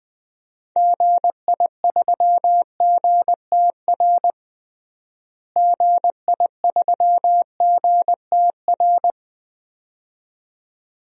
Morse code, heard as a keyed tone, about 20 wpm, 700 Hz. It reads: GI3GTR GI3GTR
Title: DXCC-100 Call Signs - 6 Characters - Repeated 2x - 20wpm